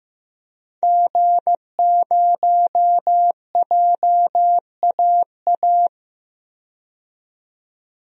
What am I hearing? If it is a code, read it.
Morse code, heard as a keyed tone, about 15 wpm, 700 Hz. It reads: G0JAA
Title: DXCC-100 Call Signs - 5 Characters - 15wpm